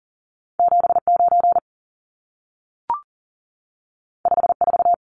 Morse code, heard as a keyed tone, about 40 wpm, 700 Hz. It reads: 79 54
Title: Numbers - 2-Digit 40wpm